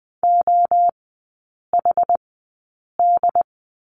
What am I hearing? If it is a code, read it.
Morse code, heard as a keyed tone, about 20 wpm, spaced as for 8 wpm, 700 Hz. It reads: OHD
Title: 138 Lesson 23e - Letter f - Three Characters 20wpm